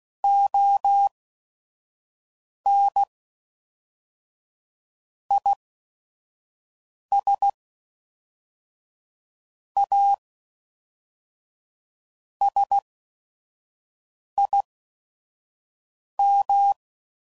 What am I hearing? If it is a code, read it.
Morse code, heard as a keyed tone, about 16 wpm, spaced as for 5 wpm, 800 Hz. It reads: ON IS A SIM
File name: CW Code Groups @5WPM / Technical Subjects / SwPwrSup